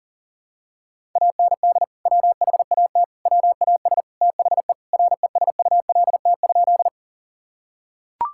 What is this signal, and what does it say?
Morse code, heard as a keyed tone, about 40 wpm, 700 Hz. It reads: AND WHAT WAS THE RESULT?
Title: Sentences from Top 1000 Words - Review Part 5 - 40wpm